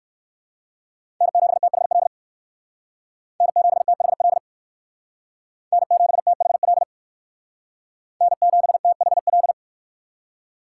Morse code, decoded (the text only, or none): N7THL N7THL N7THL N7THL